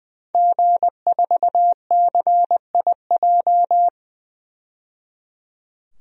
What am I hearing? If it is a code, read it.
Morse code, heard as a keyed tone, about 21 wpm, 700 Hz. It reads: G4CIJ
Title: DXCC-100 Call Signs - 5 Characters - 20wpm